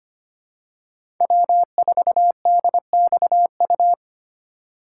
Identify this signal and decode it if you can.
Morse code, heard as a keyed tone, about 25 wpm, 700 Hz. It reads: W4DXU
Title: USA Call Signs - 5 Characters - 4x Character Spacing - 25wpm